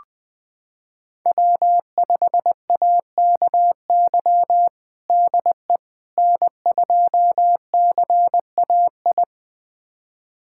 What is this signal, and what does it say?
Morse code, heard as a keyed tone, about 19 wpm, 700 Hz. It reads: W5AKY DE N2CAI
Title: USA Callsign DE Callsign - 5 Characters - 20wpm